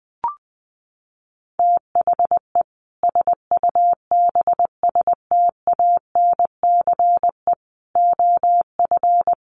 Morse code, heard as a keyed tone, about 20 wpm, 700 Hz. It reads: THE SUBSTANCE OF
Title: Sentences from Top 1500 Words 20wpm